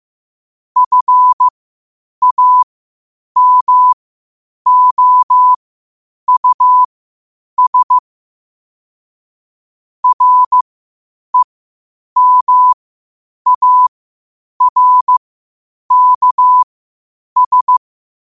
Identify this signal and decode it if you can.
Morse code, heard as a keyed tone, about 15 wpm, spaced as for 8 wpm, 1000 Hz. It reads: FAMOUS REMARKS
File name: qotdfiles / 8wpm